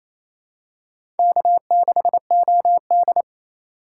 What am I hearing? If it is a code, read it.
Morse code, heard as a keyed tone, about 28 wpm, 700 Hz. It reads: K6OD
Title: USA Call Signs - 4 Characters - 4x Character Spacing - 28wpm